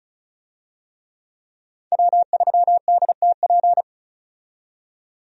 Morse code, heard as a keyed tone, about 34 wpm, 700 Hz. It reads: W3DTP
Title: USA Call Signs - 5 Characters - 35wpm